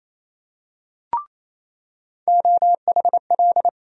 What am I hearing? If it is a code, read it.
Morse code, heard as a keyed tone, about 28 wpm, 700 Hz. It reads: OHL
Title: USA Call Sign Postfixes - 3 Letters 28wpm